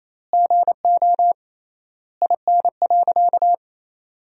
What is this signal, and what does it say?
Morse code, heard as a keyed tone, about 28 wpm, 700 Hz. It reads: GO IN.
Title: 3x Word-Spacing - Sentences from Top 400 Words - 28wpm